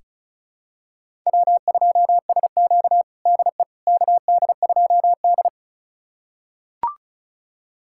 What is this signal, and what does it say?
Morse code, heard as a keyed tone, about 36 wpm, 700 Hz. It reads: W2SQ DE KD2D
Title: USA Callsign DE Callsign - 4 Characters - 35wpm